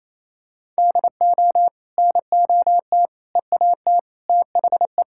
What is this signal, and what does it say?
Morse code, heard as a keyed tone, about 28 wpm, 700 Hz. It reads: DO NOT EAT THE
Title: Sentences from Top 1000 Words - Review Part 5 - 28wpm